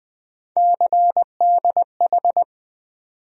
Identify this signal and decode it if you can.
Morse code, heard as a keyed tone, about 20 wpm, 700 Hz. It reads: CDH
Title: USA Call Sign Postfixes - 3 Letters 20wpm